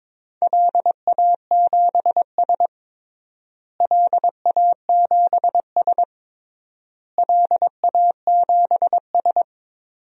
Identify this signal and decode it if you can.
Morse code, heard as a keyed tone, about 22 wpm, 700 Hz. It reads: LA7S LA7S LA7S
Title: DXCC-100 Call Signs - 4 Characters - Repeated 3x - 22wpm